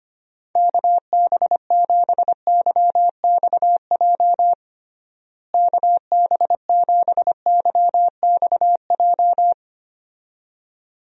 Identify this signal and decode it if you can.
Morse code, heard as a keyed tone, about 25 wpm, 700 Hz. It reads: KB7YXJ KB7YXJ
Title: USA Call Signs - 6 Characters - Repeated 2x - 25wpm